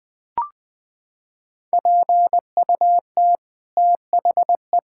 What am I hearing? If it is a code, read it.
Morse code, heard as a keyed tone, about 20 wpm, 700 Hz. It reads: PUT THE
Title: Sets of 9 Words - 20wpm